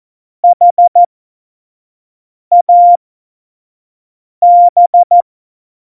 Morse code, heard as a keyed tone, about 14 wpm, spaced as for 5 wpm, 700 Hz. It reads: HAB
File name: Morsekurs / Witze / Witz 10.Juli 2018, 25 F70